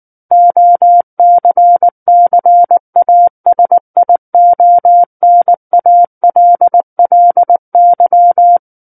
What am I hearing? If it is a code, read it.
Morse code, heard as a keyed tone, about 19 wpm, 700 Hz. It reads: OCCASIONALLY
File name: Book One / 10-Chapter 10 - 19 WPM